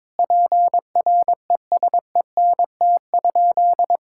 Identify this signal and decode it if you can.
Morse code, heard as a keyed tone, about 22 wpm, 700 Hz. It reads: PRESENT?
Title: Sentences from Top 1000 Words - Review Part 6 - 22wpm